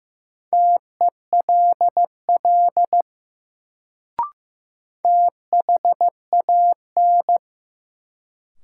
Morse code, heard as a keyed tone, about 15 wpm, 700 Hz. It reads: TELL THAN
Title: Sets of 2 Words - New Word Drill - 15wpm